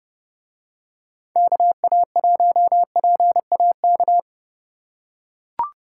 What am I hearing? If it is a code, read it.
Morse code, heard as a keyed tone, about 32 wpm, 700 Hz. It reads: KA1PAK